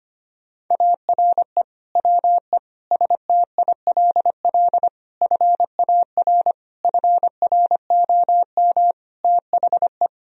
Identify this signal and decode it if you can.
Morse code, heard as a keyed tone, about 25 wpm, 700 Hz. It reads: ARE WE STILL FAR FROM THE
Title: Sentences from Top 1000 Words - Review Part 4 - 25wpm